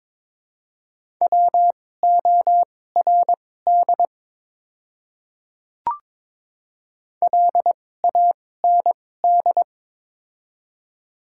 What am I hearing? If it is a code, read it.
Morse code, heard as a keyed tone, about 22 wpm, spaced as for 16 wpm, 700 Hz. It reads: WORD LAND
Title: 2x Character Spacing - Top 500 Words - 22wpm